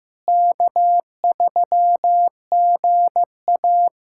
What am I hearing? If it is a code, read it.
Morse code, heard as a keyed tone, about 15 wpm, 700 Hz. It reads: K3GA
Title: USA Callsign DE Callsign - 4 Characters - 15wpm